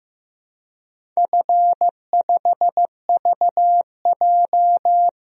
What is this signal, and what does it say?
Morse code, heard as a keyed tone, about 15 wpm, 700 Hz. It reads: F5VJ
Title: DXCC-100 Call Signs - 4 Characters - Repeated 2x - 15wpm